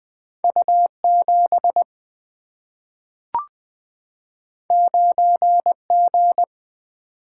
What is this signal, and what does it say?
Morse code, heard as a keyed tone, about 20 wpm, 700 Hz. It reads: U7 9G
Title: Single-Letter Single-Number 20wpm